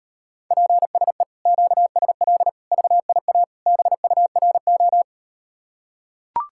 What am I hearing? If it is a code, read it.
Morse code, heard as a keyed tone, about 38 wpm, 700 Hz. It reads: PSE QSL VIA BURO